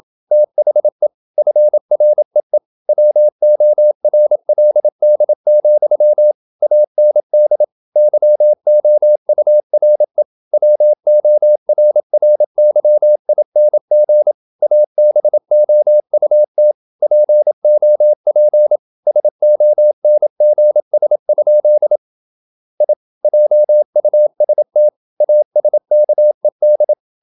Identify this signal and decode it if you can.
Morse code, heard as a keyed tone, about 28 wpm, 600 Hz. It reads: THE FREE WORLD, AND YOURE WORRYING ABOUT POP SONGS? I JUST ASKED
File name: Never 0027wpm